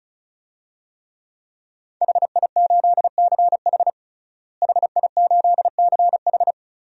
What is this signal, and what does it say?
Morse code, heard as a keyed tone, about 36 wpm, 700 Hz. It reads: HI8CH HI8CH
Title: DXCC-100 Call Signs - 5 Characters - Repeated 3x - 35wpm